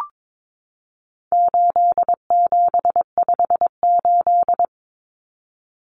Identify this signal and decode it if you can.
Morse code, heard as a keyed tone, about 22 wpm, 700 Hz. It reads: 8758
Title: Numbers - 4-Digit 22wpm